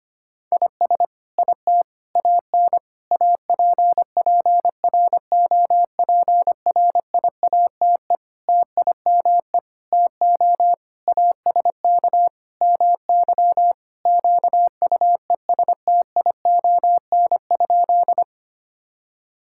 Morse code, heard as a keyed tone, about 25 wpm, 700 Hz. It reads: IS IT AN APPROPRIATE TIME TO ASK MY QUESTION?
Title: Sentences from Top 1000 Words - Review Part 5 - 25wpm